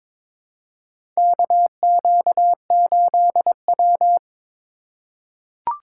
Morse code, heard as a keyed tone, about 22 wpm, 700 Hz. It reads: KQ8W
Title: USA Call Signs - 4 Characters - 3x Character Spacing - 22wpm